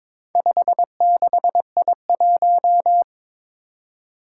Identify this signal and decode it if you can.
Morse code, heard as a keyed tone, about 22 wpm, 700 Hz. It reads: 56I1
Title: Four Challenging Letter-Number Combinations 22wpm